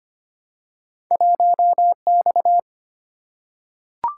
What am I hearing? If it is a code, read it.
Morse code, heard as a keyed tone, about 25 wpm, 700 Hz. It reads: 1X
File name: Two Letter-Number Combinations - 25wpm